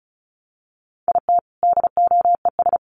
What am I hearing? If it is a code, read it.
Morse code, heard as a keyed tone, about 34 wpm, 700 Hz. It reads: IT DOES
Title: Sets of 5 Words Encore2 35wpm